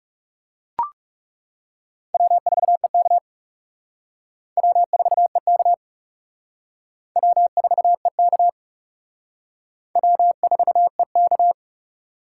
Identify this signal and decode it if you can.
Morse code, heard as a keyed tone, about 37 wpm, 700 Hz. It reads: W4EK W4EK W4EK W4EK
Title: Speed Racing - USA 1x2 Call Signs 30wpm